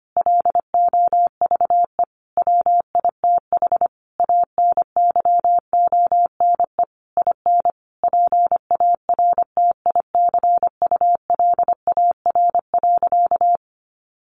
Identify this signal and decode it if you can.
Morse code, heard as a keyed tone, about 25 wpm, 700 Hz. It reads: LOVE WITH ANYONE IN PARTICULAR.
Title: Sentences from Top 900 Words 25wpm